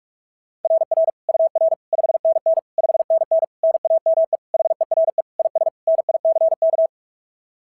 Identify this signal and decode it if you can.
Morse code, heard as a keyed tone, about 46 wpm, 690 Hz. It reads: RR UR 5NN 5NN NAME HERE IS NICK
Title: SOTA QSO Practice 45wpm